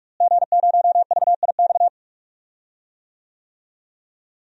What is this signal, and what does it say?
Morse code, heard as a keyed tone, about 46 wpm, 700 Hz. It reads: G0VIX
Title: DXCC-100 Call Signs - 5 Characters - Repeated 3x - 45wpm